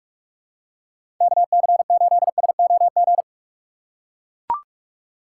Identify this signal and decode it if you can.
Morse code, heard as a keyed tone, about 41 wpm, 700 Hz. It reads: KC8SOG
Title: USA Call Signs - 6 Characters - 3x Character Spacing - 45wpm